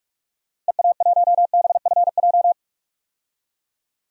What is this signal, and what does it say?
Morse code, heard as a keyed tone, about 46 wpm, 700 Hz. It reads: EA1BFJ
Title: DXCC-100 Call Signs - 6 Characters - 45wpm